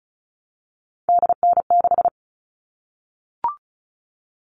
Morse code, heard as a keyed tone, about 34 wpm, 700 Hz. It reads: DN6
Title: All Cut Numbers - 3-Digit 35wpm